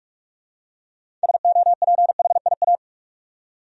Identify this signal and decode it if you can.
Morse code, heard as a keyed tone, about 45 wpm, 700 Hz. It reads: SOPHIA
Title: USA Common Names - 45wpm